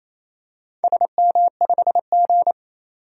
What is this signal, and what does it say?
Morse code, heard as a keyed tone, about 27 wpm, 700 Hz. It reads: SM5G
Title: DXCC-100 Call Signs - 4 Characters - 28wpm